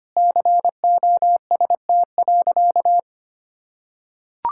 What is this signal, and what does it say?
Morse code, heard as a keyed tone, about 25 wpm, 700 Hz. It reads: COST.
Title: Sentences from Top 1000 Words - Review Part 5 - 25wpm